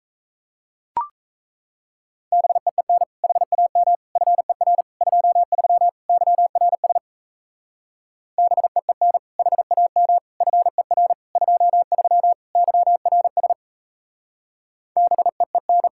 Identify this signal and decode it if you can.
Morse code, heard as a keyed tone, about 34 wpm, 700 Hz. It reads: BEEN HAM FER 23 YRS BEEN HAM FER 23 YRS BEEN
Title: Speed-Racing Common QSO Phrases - 28wpm